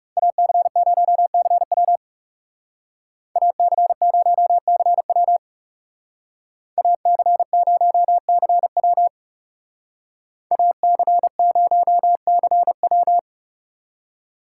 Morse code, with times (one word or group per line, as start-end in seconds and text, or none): 0.2-2.0 AC0CW
3.4-5.4 AC0CW
6.8-9.1 AC0CW
10.5-13.2 AC0CW